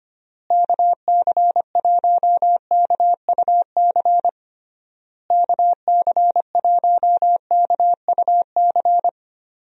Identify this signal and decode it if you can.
Morse code, heard as a keyed tone, about 25 wpm, 700 Hz. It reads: KC1KUC KC1KUC
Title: USA Call Signs - 6 Characters - Repeated 3x - 25wpm